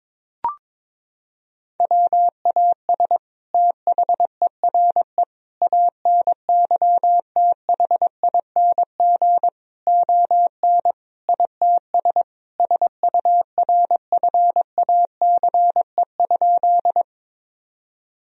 Morse code, heard as a keyed tone, about 22 wpm, 700 Hz. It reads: WAS THERE ANYTHING ON ITS SURFACE?
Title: Sentences from Top 1000 Words - Review Part 3 - 22wpm